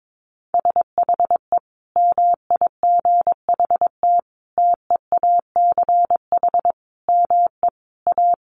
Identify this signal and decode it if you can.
Morse code, heard as a keyed tone, about 22 wpm, 700 Hz. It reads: SHE MIGHT TEACH ME A
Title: Sentences from Top 1200 Words 22wpm